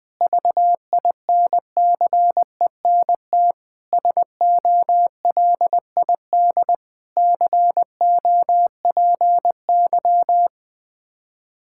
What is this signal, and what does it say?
Morse code, heard as a keyed tone, about 20 wpm, 700 Hz. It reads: VINCENT SOLID COPY
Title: Practice QSO - Exchange 3 - More Info - 20wpm